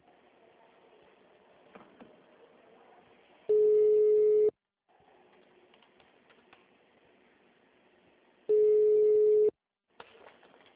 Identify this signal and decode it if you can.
Ringback tone (Europe, 425 Hz).
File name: out- -2007- - - .166